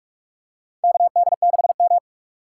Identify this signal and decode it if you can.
Morse code, heard as a keyed tone, about 44 wpm, 700 Hz. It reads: KD6M
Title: USA Call Signs - 4 Characters - 6x Character Spacing - 45wpm